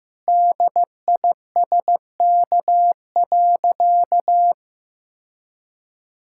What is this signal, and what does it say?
Morse code, heard as a keyed tone, about 16 wpm, 700 Hz. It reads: DISK.
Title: Sentences from Top 1000 Words - Review Part 6 - 15wpm